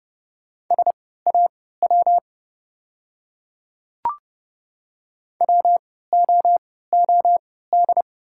Morse code, heard as a keyed tone, about 30 wpm, spaced as for 17 wpm, 690 Hz. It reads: SAW WOOD
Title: 3x Character Spacing - Top 500 Words - 30wpm